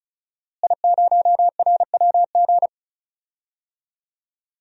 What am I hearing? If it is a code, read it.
Morse code, heard as a keyed tone, about 36 wpm, 700 Hz. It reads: I0RWG
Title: DXCC-100 Call Signs - 5 Characters - Repeated 2x - 35wpm